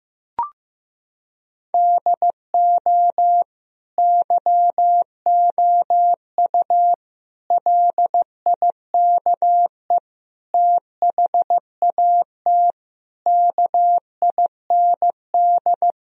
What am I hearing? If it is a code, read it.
Morse code, heard as a keyed tone, about 15 wpm, 700 Hz. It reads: DO YOU LIKE THAT KIND
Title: Sentences from Top 1000 Words - Review Part 4 - 15wpm